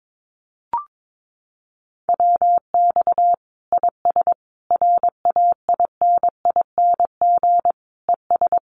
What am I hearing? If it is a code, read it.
Morse code, heard as a keyed tone, about 22 wpm, 700 Hz. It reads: WX IS RAINING ES
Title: Common QSO Phrases 22wpm